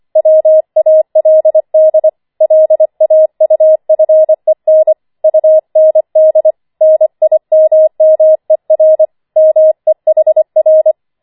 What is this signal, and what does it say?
Morse code, heard as a keyed tone, about 25 wpm, 630 Hz. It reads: WALD LAUFEN UND NIMMER MEHR
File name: 25wpm